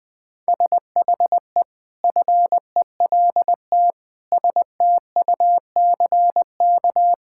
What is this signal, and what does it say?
Morse code, heard as a keyed tone, about 20 wpm, 700 Hz. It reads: SHE FELT STUCK
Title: Sentences with 8-Word Phrases - 20wpm